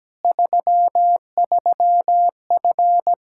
Morse code, heard as a keyed tone, about 17 wpm, 700 Hz. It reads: 33F
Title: Practice QSO - Exchange 2 - Info - 17wpm